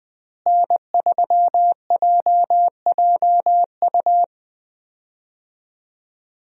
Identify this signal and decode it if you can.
Morse code, heard as a keyed tone, about 20 wpm, 700 Hz. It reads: N3JJU
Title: USA Call Signs - 5 Characters - Repeated 2x - 20wpm